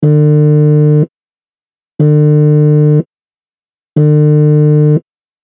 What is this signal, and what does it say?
There is no code.